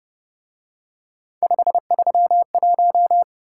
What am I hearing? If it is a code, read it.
Morse code, heard as a keyed tone, about 31 wpm, 700 Hz. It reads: 531